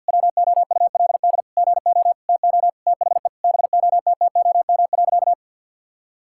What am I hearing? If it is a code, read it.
Morse code, heard as a keyed tone, about 46 wpm, 700 Hz. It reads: WOULD GO TO THE BOTTOM.
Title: Sentences from Top 1000 Words - Review Part 6 - 50wpm